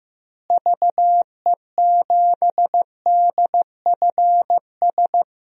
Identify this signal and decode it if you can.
Morse code, heard as a keyed tone, about 15 wpm, 700 Hz. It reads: VE7DFS
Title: DXCC-100 Call Signs - 6 Characters - Repeated 3x - 15wpm